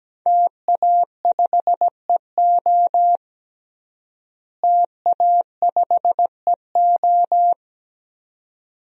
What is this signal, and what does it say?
Morse code, heard as a keyed tone, about 17 wpm, 700 Hz. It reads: TA5EO TA5EO
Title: DXCC-100 Call Signs - 5 Characters - Repeated 2x - 17wpm